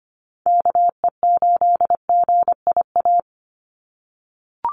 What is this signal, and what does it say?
Morse code, heard as a keyed tone, about 25 wpm, 700 Hz. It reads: KE8GIA
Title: USA Call Signs 25wpm